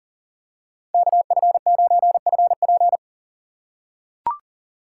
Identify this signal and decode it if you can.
Morse code, heard as a keyed tone, about 40 wpm, 700 Hz. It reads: KF9FP
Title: USA Call Signs - 4x Character Spacing - 40wpm